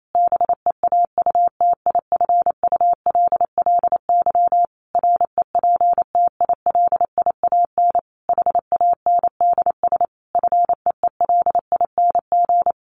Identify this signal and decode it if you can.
Morse code, heard as a keyed tone, about 28 wpm, 700 Hz. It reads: BEAUTIFULLY REPTILIAN HANDS FEELING
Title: A Sound Of Thunder 28wpm